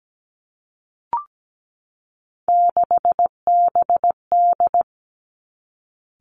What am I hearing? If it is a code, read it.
Morse code, heard as a keyed tone, about 17 wpm, 700 Hz. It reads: 6BD
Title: Three Challenging Letter-Number Combinations 17wpm